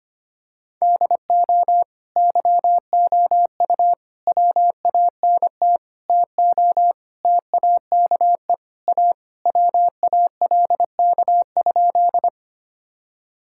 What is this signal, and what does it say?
Morse code, heard as a keyed tone, about 25 wpm, 700 Hz. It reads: DO YOU WANT TO TAKE A WALK?
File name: practice / 25wpm / Sentences with 3-Word Phrases - 25wpm